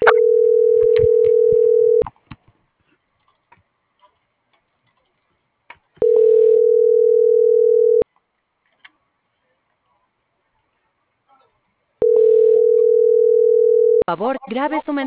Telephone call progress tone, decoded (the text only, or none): ringback tone